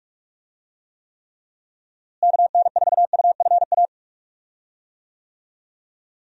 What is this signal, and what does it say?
Morse code, heard as a keyed tone, about 46 wpm, 700 Hz. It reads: KN4UFA
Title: USA Call Signs - 6 Characters - Repeated 3x - 45wpm